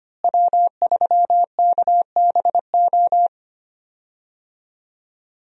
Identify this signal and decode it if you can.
Morse code, heard as a keyed tone, about 25 wpm, 700 Hz. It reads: W3KBO
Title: USA Call Signs - 5 Characters - Repeated 3x - 25wpm